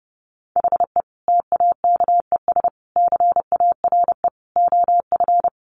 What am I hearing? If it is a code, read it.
Morse code, heard as a keyed tone, about 30 wpm, 700 Hz. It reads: HE TAKES CARE OF
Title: Additional Easy Phrases 30wpm